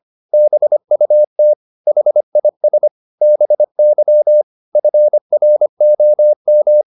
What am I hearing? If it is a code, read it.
Morse code, heard as a keyed tone, about 25 wpm, 600 Hz. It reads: BUT HIS BY FROM
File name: Top-100-Words-25